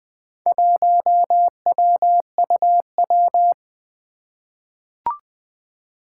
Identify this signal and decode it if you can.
Morse code, heard as a keyed tone, about 20 wpm, 700 Hz. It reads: 1WUW
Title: Four Letter-Number Combinations - 20wpm